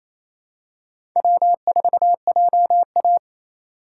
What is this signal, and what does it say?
Morse code, heard as a keyed tone, about 27 wpm, 700 Hz. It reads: W4JA
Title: USA Call Signs - 4 Characters - 2x Character Spacing - 28wpm